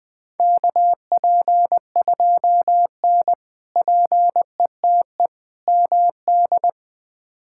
Morse code, heard as a keyed tone, about 20 wpm, 700 Hz. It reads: KP2N PETE MD
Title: K1USN SST QSO Practice - 20wpm